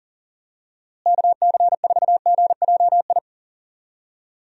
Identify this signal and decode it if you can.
Morse code, heard as a keyed tone, about 40 wpm, 700 Hz. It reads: KC4GJI
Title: USA Call Signs - 6 Characters - Repeated 2x - 40wpm